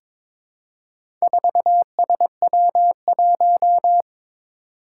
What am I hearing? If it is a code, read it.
Morse code, heard as a keyed tone, about 23 wpm, 700 Hz. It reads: 4SW1
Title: Four Challenging Letter-Number Combinations 22wpm